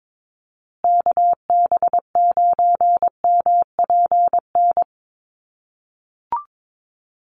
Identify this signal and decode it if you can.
Morse code, heard as a keyed tone, about 22 wpm, 700 Hz. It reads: KB9MPN
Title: USA Call Signs Repeated 3x 22wpm